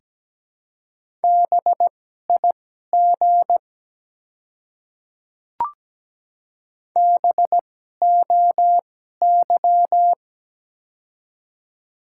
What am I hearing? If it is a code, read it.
Morse code, heard as a keyed tone, about 17 wpm, spaced as for 12 wpm, 700 Hz. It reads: BIG BOY
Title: 2x Character Spacing - Top 200 Words - 17wpm